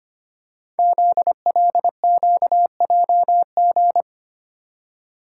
Morse code, heard as a keyed tone, about 25 wpm, 700 Hz. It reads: ZLQJG
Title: Code Groups - 5 Letters - 25wpm